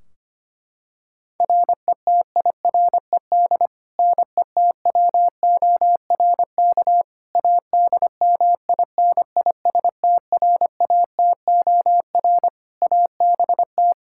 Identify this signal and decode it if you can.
Morse code, heard as a keyed tone, about 25 wpm, 700 Hz. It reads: RETIRED NETWORK ADMINISTRATOR ABT
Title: K1USN SST QSO Practice 25wpm